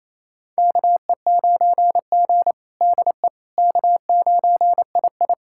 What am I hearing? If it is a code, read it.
Morse code, heard as a keyed tone, about 28 wpm, 700 Hz. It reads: KE9G DE K9II
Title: USA Callsign DE Callsign - 4 Characters - 28wpm